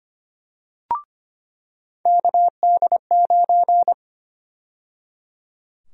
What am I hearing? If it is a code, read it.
Morse code, heard as a keyed tone, about 25 wpm, 700 Hz. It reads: KD9
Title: USA Call Sign Prefixes - 25wpm